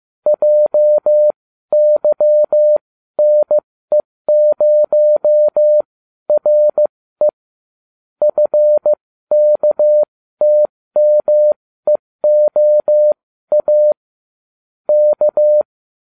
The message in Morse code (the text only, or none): JYNE0RE FKTMEOA K